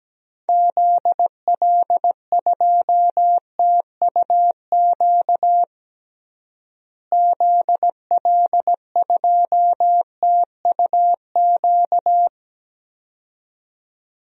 Morse code, heard as a keyed tone, about 17 wpm, 700 Hz. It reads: ZL2TUQ ZL2TUQ
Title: DXCC-100 Call Signs - 6 Characters - Repeated 2x - 17wpm